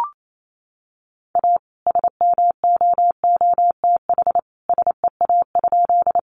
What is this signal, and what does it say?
Morse code, heard as a keyed tone, about 28 wpm, 700 Hz. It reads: A SMOOTH SEA?
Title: Sentences from Top 1400 Words 28wpm